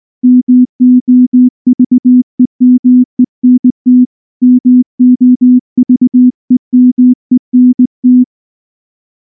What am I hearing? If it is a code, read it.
Morse code, heard as a keyed tone, about 18 wpm, 250 Hz. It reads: MOVEMENT MOVEMENT
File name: morse / morse-training-20WPM-6